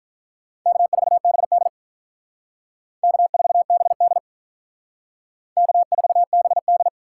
Morse code, heard as a keyed tone, about 45 wpm, 700 Hz. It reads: K4BD K4BD K4BD